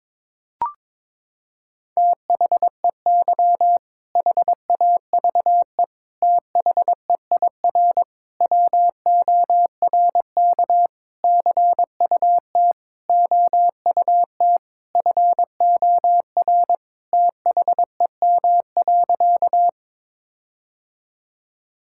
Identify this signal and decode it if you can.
Morse code, heard as a keyed tone, about 22 wpm, 700 Hz. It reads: THEY HAVE THEIR WORK CUT OUT FOR THEM.
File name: Sentences with 7-Word Phrases - 22wpm